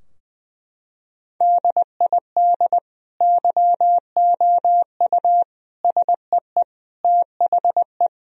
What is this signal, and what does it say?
Morse code, heard as a keyed tone, about 21 wpm, 700 Hz. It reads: DID YOU SEE THE
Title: Sentences from Top 1000 Words - Review Part 2 - 20wpm